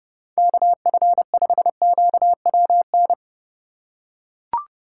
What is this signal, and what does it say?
Morse code, heard as a keyed tone, about 32 wpm, 700 Hz. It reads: KF5QWN
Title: USA Call Signs - 6 Characters - 2x Character Spacing - 30wpm